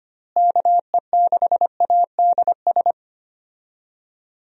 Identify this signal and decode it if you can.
Morse code, heard as a keyed tone, about 25 wpm, 700 Hz. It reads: KE6ADS
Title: USA Call Signs - 6 Characters - Repeated 2x - 25wpm